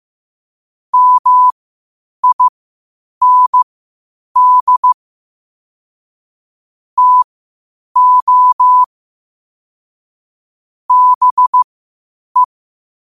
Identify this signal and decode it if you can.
Morse code, heard as a keyed tone, about 15 wpm, spaced as for 8 wpm, 1000 Hz. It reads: MIND TO BE
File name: qotdfiles / 8wpm